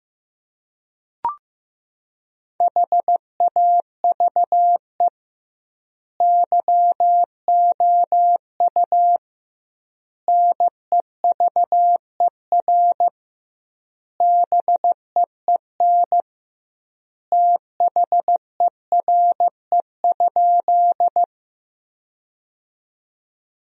Morse code, heard as a keyed tone, about 15 wpm, 700 Hz. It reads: HAVE YOU NEVER BEEN THERE?
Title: 2x Word-Spacing - Sentences from Top 300 Words - 15wpm